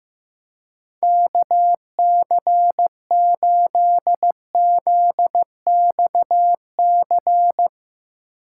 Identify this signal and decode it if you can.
Morse code, heard as a keyed tone, about 16 wpm, 700 Hz. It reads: KC8ZXC
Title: USA Call Signs - 6 Characters - 4x Character Spacing - 15wpm